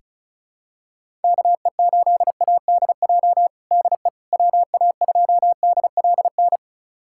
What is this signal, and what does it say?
Morse code, heard as a keyed tone, about 34 wpm, 700 Hz. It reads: KE8ADJ DE WA2DLN
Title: USA Callsign DE Callsign - 6 Characters - 35wpm